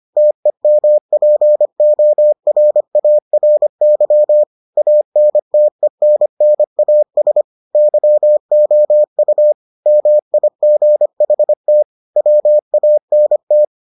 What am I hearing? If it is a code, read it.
Morse code, heard as a keyed tone, about 25 wpm, 600 Hz. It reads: TEMPORARY ANTENNAS YOU MIGHT WANT